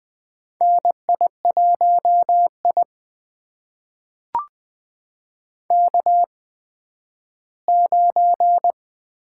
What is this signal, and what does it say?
Morse code, heard as a keyed tone, about 20 wpm, spaced as for 12 wpm, 700 Hz. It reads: NI1I K9
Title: USA Call Signs - 4 Characters - 8x Character Spacing - 20wpm